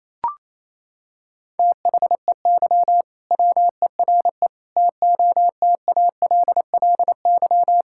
Morse code, heard as a keyed tone, about 28 wpm, 700 Hz. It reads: THEY WERE TOTALLY
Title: Sentences from Top 1600 Words 28wpm